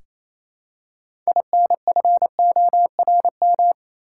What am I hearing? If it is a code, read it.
Morse code, heard as a keyed tone, about 28 wpm, 700 Hz. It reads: INFORM
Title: Six Letter Words - 28wpm